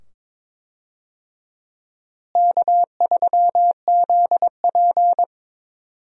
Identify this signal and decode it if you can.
Morse code, heard as a keyed tone, about 22 wpm, 700 Hz. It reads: K3ZP
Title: USA Call Signs - 4 Characters - Repeated 2x - 22wpm